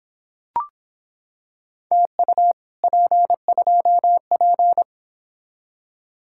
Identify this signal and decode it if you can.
Morse code, heard as a keyed tone, about 26 wpm, 700 Hz. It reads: TU P2P